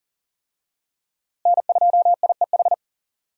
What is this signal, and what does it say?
Morse code, heard as a keyed tone, about 40 wpm, 700 Hz. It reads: N2IEH
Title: USA Call Signs - 5 Characters - Repeated 3x - 40wpm